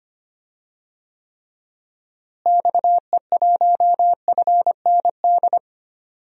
Morse code, heard as a keyed tone, about 25 wpm, 700 Hz. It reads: XE1FND
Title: DXCC-100 Call Signs - 6 Characters - Repeated 3x - 25wpm